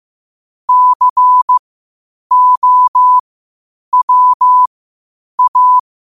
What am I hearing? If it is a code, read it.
Morse code, heard as a keyed tone, about 15 wpm, spaced as for 8 wpm, 1000 Hz. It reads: COWA
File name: qotdfiles / 8wpm